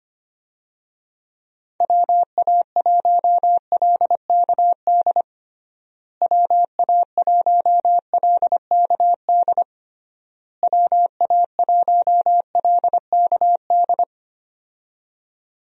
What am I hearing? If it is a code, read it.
Morse code, heard as a keyed tone, about 25 wpm, 700 Hz. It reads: WA1LKD WA1LKD WA1LKD
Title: USA Call Signs - 6 Characters - Repeated 3x - 25wpm